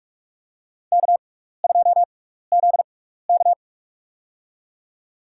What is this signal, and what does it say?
Morse code, heard as a keyed tone, about 46 wpm, spaced as for 16 wpm, 700 Hz. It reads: K2ZK
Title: USA Call Signs - 4 Characters - 6x Character Spacing - 45wpm